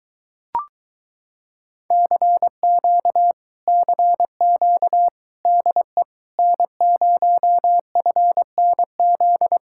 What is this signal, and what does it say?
Morse code, heard as a keyed tone, about 23 wpm, 700 Hz. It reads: CQ CQ DE N0FNZ